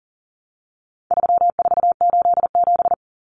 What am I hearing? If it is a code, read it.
Morse code, heard as a keyed tone, about 40 wpm, 700 Hz. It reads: 3487
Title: Numbers - 4-Digit 40wpm